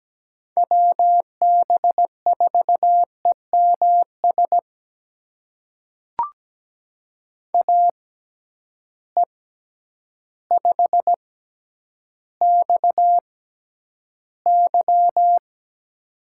Morse code, heard as a keyed tone, about 17 wpm, spaced as for 9 wpm, 700 Hz. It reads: WB4EMS AE5XY
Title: USA Call Signs - 6x Character Spacing - 17wpm